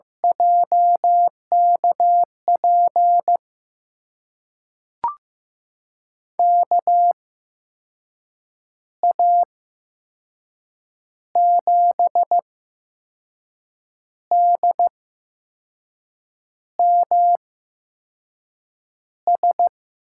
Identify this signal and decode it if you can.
Morse code, heard as a keyed tone, about 15 wpm, spaced as for 5 wpm, 700 Hz. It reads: JKP KA7DMS
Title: USA Call Signs - 6 Characters - 8x Character Spacing - 15wpm